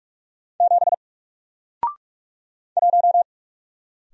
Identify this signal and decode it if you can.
Morse code, heard as a keyed tone, about 44 wpm, 700 Hz. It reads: 7 1
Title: Single Character (Letter:Number:Punctuation:Prosign) 45wpm